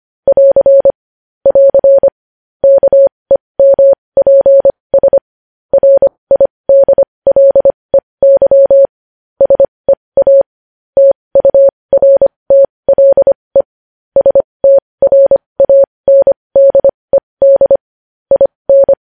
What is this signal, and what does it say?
Morse code, heard as a keyed tone, about 25 wpm, spaced as for 20 wpm, 550 Hz. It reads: <AR> <AR> KEMPS RIDLEY SEA TURTLE STRANDED IN
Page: news-headlines-in-morse-code-at-20-wpm